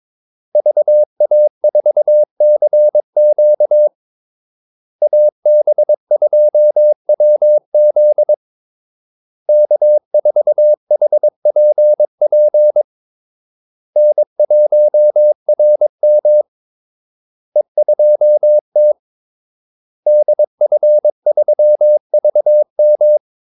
Morse code, heard as a keyed tone, about 22 wpm, 600 Hz. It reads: VA4CQ AB2WZ K4HPP N1RM E2T DF3VM
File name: Callsigns-22wpm